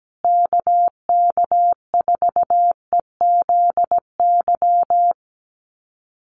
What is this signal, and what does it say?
Morse code, heard as a keyed tone, about 17 wpm, 700 Hz. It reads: KK4EZY
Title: USA Call Signs Repeated 2x 17wpm